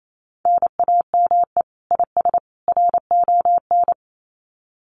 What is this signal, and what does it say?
Morse code, heard as a keyed tone, about 28 wpm, 700 Hz. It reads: NAME IS RON
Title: Common QSO Phrases 28wpm